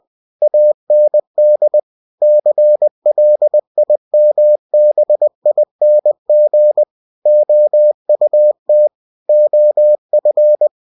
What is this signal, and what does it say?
Morse code, heard as a keyed tone, about 20 wpm, 600 Hz. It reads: AND CLIMBING OUT OF
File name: cw-books / winnie-the-pooh / winnie-20wpm-page-0066